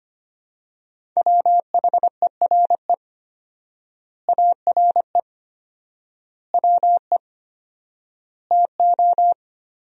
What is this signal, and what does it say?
Morse code, heard as a keyed tone, about 25 wpm, 700 Hz. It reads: WHERE ARE WE TO
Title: 4x Word-Spacing - Sentences from Top 200 Words - 25wpm